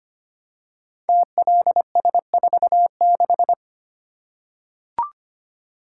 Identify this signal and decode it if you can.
Morse code, heard as a keyed tone, about 25 wpm, 700 Hz. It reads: TLS46
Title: Code Groups - 5 Letter:Numbers - 25wpm